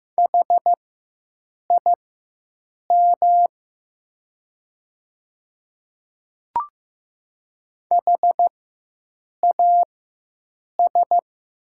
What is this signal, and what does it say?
Morse code, heard as a keyed tone, about 15 wpm, spaced as for 7 wpm, 700 Hz. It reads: HIM HAS
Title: 4x Character Spacing - Top 100 Words - 15wpm